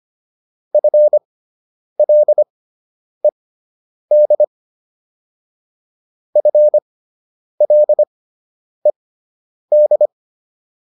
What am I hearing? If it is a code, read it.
Morse code, heard as a keyed tone, about 25 wpm, spaced as for 9 wpm, 600 Hz. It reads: FLED FLED